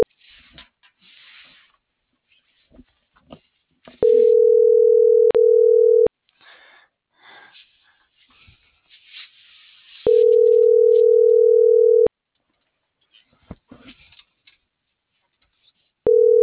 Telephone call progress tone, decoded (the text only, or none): ringback tone